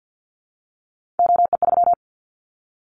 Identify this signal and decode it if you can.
Morse code, heard as a keyed tone, about 50 wpm, 700 Hz. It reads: OE3
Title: Callsign Prefixes for Top 25 DXCC - 50wpm